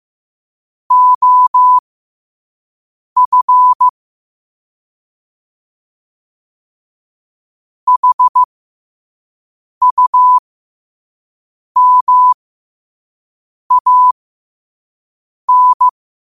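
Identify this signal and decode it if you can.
Morse code, heard as a keyed tone, about 15 wpm, spaced as for 5 wpm, 1000 Hz. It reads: OF HUMAN
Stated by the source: Morse sent at 5 wpm.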